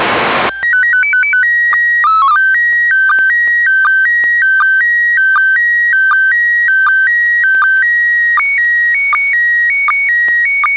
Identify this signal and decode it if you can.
SSTV (slow-scan television), PD 180.
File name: PY4ZBZ ISS-SSTV Audio 12h39